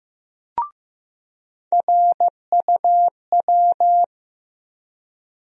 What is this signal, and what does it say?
Morse code, heard as a keyed tone, about 15 wpm, 700 Hz. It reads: RUW
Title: USA Call Sign Postfixes - 3 Letters 15wpm